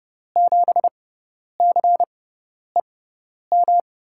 Morse code, heard as a keyed tone, about 30 wpm, spaced as for 10 wpm, 700 Hz. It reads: 7CEM
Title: USA Call Signs - 6x Character Spacing - 30wpm